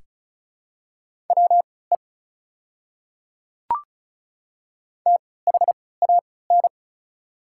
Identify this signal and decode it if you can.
Morse code, heard as a keyed tone, about 36 wpm, spaced as for 20 wpm, 700 Hz. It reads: WE THAN
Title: 3x Character Spacing - Top 100 Words - 35wpm